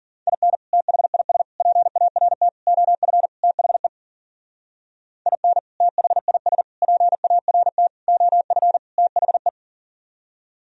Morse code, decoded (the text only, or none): IN THIS PART OF THE IN THIS PART OF THE